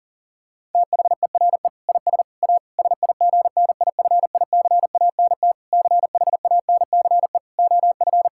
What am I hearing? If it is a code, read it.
Morse code, heard as a keyed tone, about 40 wpm, 700 Hz. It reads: THERE IS A SIGNIFICANT CHANCE OF
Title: Sentences from Top 1000 Words - Review Part 6 - 40wpm